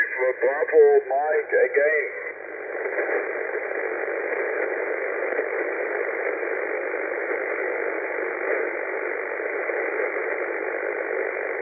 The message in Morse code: none